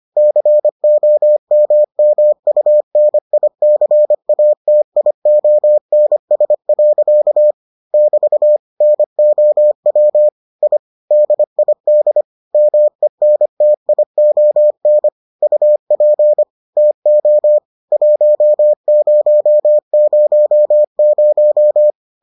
Morse code, heard as a keyed tone, about 25 wpm, 600 Hz. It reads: COMMUNICATIONS. <BT> NOW I DID MENTION UP TO 1000